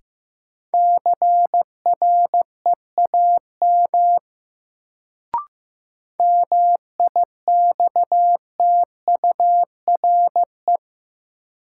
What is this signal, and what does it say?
Morse code, heard as a keyed tone, about 15 wpm, 700 Hz. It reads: CREAM MIXTURE
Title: Sentences from Top 1500 Words - New Word Drill Alt - 15wpm